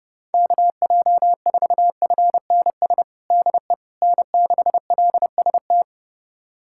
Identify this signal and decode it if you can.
Morse code, heard as a keyed tone, about 30 wpm, 700 Hz. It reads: KJ4FNS DE N6LST
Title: USA Callsign DE Callsign 30wpm